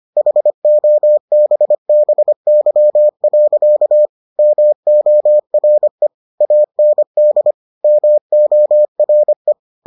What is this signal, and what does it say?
Morse code, heard as a keyed tone, about 25 wpm, 600 Hz. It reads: HOBBY. MORE AND MORE